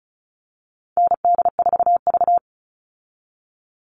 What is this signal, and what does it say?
Morse code, heard as a keyed tone, about 34 wpm, 700 Hz. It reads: ND4V
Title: Call Signs Repeated 3x - 35wpm